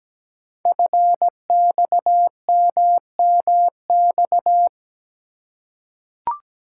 Morse code, heard as a keyed tone, about 17 wpm, 700 Hz. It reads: FXMMX
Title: Code Groups - 5 Letters - 17wpm